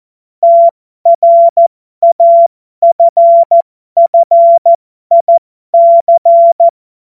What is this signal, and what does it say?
Morse code, heard as a keyed tone, about 14 wpm, spaced as for 12 wpm, 680 Hz. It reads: TRAFFIC
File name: Morsekurs / Witze